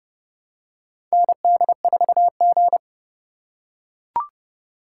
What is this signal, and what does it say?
Morse code, heard as a keyed tone, about 30 wpm, 700 Hz. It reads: ND4G
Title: K1USN SST QSO Practice - 30wpm